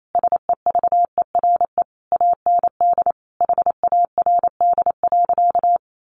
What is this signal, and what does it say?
Morse code, heard as a keyed tone, about 28 wpm, 700 Hz. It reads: SEVERE AND HARD.
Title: Sentences from Top 1400 Words 28wpm